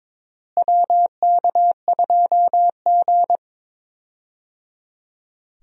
Morse code, heard as a keyed tone, about 22 wpm, 700 Hz. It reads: WK2G
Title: USA Call Signs - 4 Characters - Repeated 3x - 22wpm